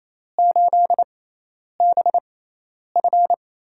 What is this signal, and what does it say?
Morse code, heard as a keyed tone, about 28 wpm, spaced as for 10 wpm, 700 Hz. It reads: 8BF
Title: USA Call Signs - 4 Characters - 6x Character Spacing - 28wpm